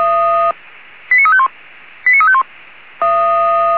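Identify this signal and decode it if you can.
Five-tone selective call (ZVEI), heard as quick runs of five tones, tones of about 70 ms. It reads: 89351 89351
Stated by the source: ZVEI five-tone selective call received on SDR.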